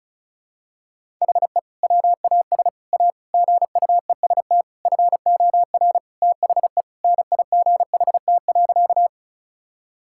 Morse code, heard as a keyed tone, about 34 wpm, 700 Hz. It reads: HE WAS A GUEST FOR THE NIGHT.
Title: Sentences from Top 1600 Words 35wpm